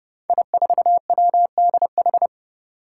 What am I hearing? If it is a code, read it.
Morse code, heard as a keyed tone, about 30 wpm, 700 Hz. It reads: I4WDH
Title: Five Challenging Letter-Number Combinations 30wpm